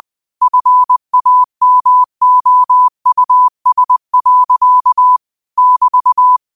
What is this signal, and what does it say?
Morse code, heard as a keyed tone, about 20 wpm, 1000 Hz. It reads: FAMOUS. <BT>
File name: qotdfiles / 20wpm